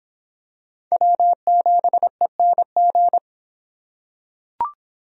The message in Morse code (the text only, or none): W7ENG